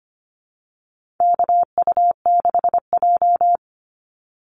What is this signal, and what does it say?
Morse code, heard as a keyed tone, about 25 wpm, 700 Hz. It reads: KU6J
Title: NA SOTA Activator Callsigns 25wpm